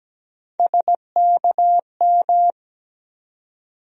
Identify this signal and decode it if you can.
Morse code, heard as a keyed tone, about 17 wpm, 700 Hz. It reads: SKM
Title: USA Call Sign Postfixes - 3 Letters 17wpm